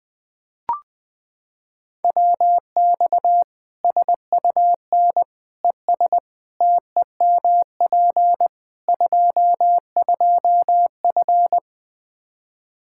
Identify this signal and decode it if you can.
Morse code, heard as a keyed tone, about 21 wpm, 700 Hz. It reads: WX SUN ES TEMP 22F
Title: Practice QSO - Exchange 2 - Info - 20wpm